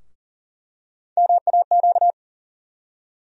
Morse code, heard as a keyed tone, about 40 wpm, 700 Hz. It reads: MAQ
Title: USA Call Sign Postfixes - 3 Letters 40wpm